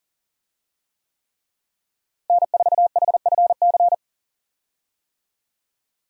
Morse code, heard as a keyed tone, about 40 wpm, 690 Hz. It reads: N4HFC
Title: USA Call Signs - 5 Characters - Repeated 3x - 40wpm